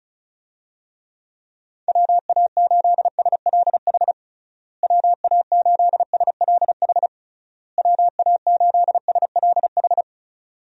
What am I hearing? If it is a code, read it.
Morse code, heard as a keyed tone, about 34 wpm, 700 Hz. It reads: WA8SLH WA8SLH WA8SLH
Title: USA Call Signs - 6 Characters - Repeated 3x - 35wpm